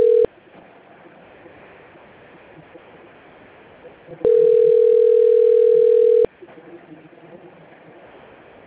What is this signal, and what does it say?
Ringback tone (North America), 440 + 480 Hz.
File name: rg-1600-unknown- - - .14024